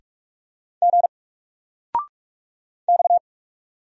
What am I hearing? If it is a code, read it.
Morse code, heard as a keyed tone, about 44 wpm, 700 Hz. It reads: G X
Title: Single Character (Letter:Number:Punctuation:Prosign) 45wpm